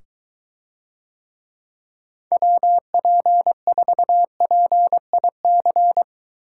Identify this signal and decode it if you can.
Morse code, heard as a keyed tone, about 22 wpm, 700 Hz. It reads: WP4PIC